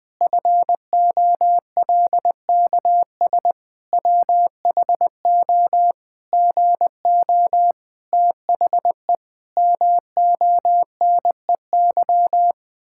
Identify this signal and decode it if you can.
Morse code, heard as a keyed tone, about 20 wpm, 700 Hz. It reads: FOLKS WHO GO THE MONEY
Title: Pangram Challenge v2 - 20wpm